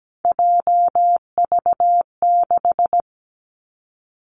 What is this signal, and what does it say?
Morse code, heard as a keyed tone, about 17 wpm, 700 Hz. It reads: JV6
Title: Three Challenging Letter-Number Combinations 17wpm